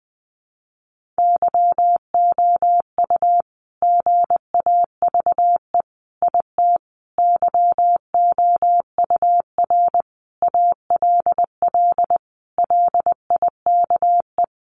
Morse code, heard as a keyed tone, about 20 wpm, 700 Hz. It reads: YOU GAVE IT YOUR ALL LIKE
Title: Sentences from Top 1500 Words 20wpm